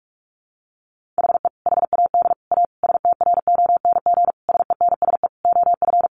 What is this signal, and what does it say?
Morse code, heard as a keyed tone, about 44 wpm, 700 Hz. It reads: HE HAD A STRONG SENSE OF
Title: Sentences from Top 1500 Words 45wpm